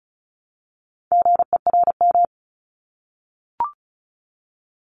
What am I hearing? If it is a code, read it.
Morse code, heard as a keyed tone, about 36 wpm, 700 Hz. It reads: GERM
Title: Multi-letter Phonemes Repeated 3x 35wpm